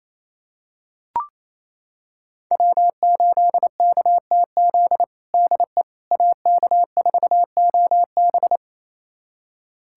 Morse code, heard as a keyed tone, about 28 wpm, 700 Hz. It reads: W8KTZ DE AK4OB
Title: USA Callsign DE Callsign - 5 Characters - 28wpm